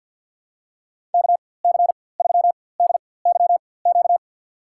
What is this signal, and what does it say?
Morse code, heard as a keyed tone, about 50 wpm, spaced as for 23 wpm, 700 Hz. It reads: KC3DYQ
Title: USA Call Signs - 6 Characters - 4x Character Spacing - 50wpm